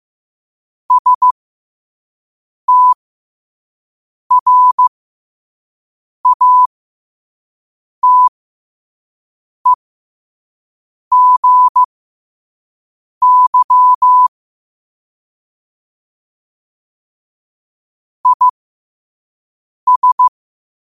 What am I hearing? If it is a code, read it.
Morse code, heard as a keyed tone, about 15 wpm, spaced as for 5 wpm, 1000 Hz. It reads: STRATEGY IS
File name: qotdfiles / 5wpm